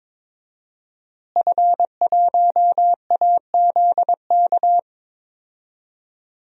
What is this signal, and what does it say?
Morse code, heard as a keyed tone, about 22 wpm, 700 Hz. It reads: F1AZK
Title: DXCC-100 Call Signs - 5 Characters - Repeated 2x - 22wpm